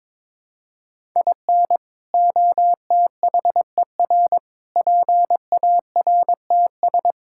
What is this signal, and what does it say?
Morse code, heard as a keyed tone, about 22 wpm, 700 Hz. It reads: IN OTHER PARTS
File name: practice / 22wpm / Build A Phrase - 5-Word Phrases - 22wpm